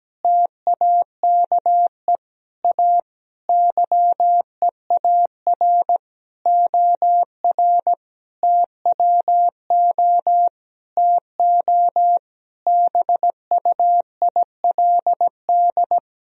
Morse code, heard as a keyed tone, about 17 wpm, 700 Hz. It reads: TAKE A YEAR OR TWO TO BUILD
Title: Sentences from Top 1000 Words - Review Part 1 - 17wpm